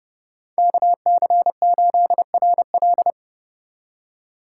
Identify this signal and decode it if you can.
Morse code, heard as a keyed tone, about 30 wpm, 700 Hz. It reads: KC8RL